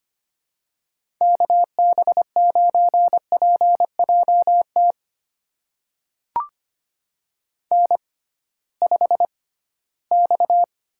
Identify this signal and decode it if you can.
Morse code, heard as a keyed tone, about 25 wpm, spaced as for 16 wpm, 700 Hz. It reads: KB9PJT N5X
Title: USA Call Signs - 6x Character Spacing - 25wpm